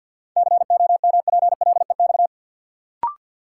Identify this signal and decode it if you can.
Morse code, heard as a keyed tone, about 53 wpm, 700 Hz. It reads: COMPLEX
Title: Sentences from Top 900 Words - New Word Drill - 50wpm